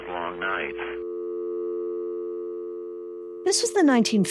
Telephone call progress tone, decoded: dial tone